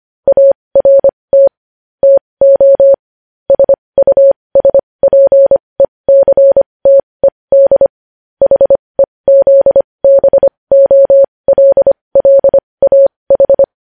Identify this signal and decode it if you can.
Morse code, heard as a keyed tone, about 25 wpm, spaced as for 20 wpm, 550 Hz. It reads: ART TO SUSPECTED HEZBOLLAH
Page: News Headlines in Morse Code at 20 WPM